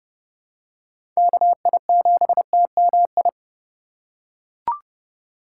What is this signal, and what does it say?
Morse code, heard as a keyed tone, about 30 wpm, 700 Hz. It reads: KI7TMI
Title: USA Call Signs - 6 Characters - 2x Character Spacing - 30wpm